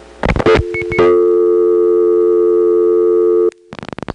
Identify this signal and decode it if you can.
Dial tone (North America), 350 + 440 Hz.
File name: phonetrips / SmyrnaGA3HQ